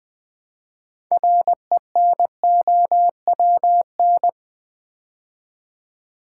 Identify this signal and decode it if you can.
Morse code, heard as a keyed tone, about 20 wpm, 700 Hz. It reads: RENOWN
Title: Build A Word - 2 Letters At A Time - 8-Letter Words - 20wpm